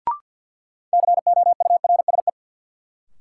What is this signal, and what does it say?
Morse code, heard as a keyed tone, about 53 wpm, 700 Hz. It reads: COURSE
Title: Sentences from Top 500 Words - New Word Drill - 50wpm